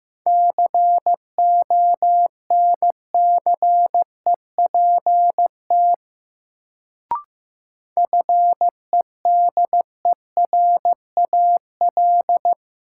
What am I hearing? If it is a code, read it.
Morse code, heard as a keyed tone, about 16 wpm, 700 Hz. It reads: CONCEPT FEDERAL
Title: Sentences from Top 1100 Words - New Word Drill Alt - 15wpm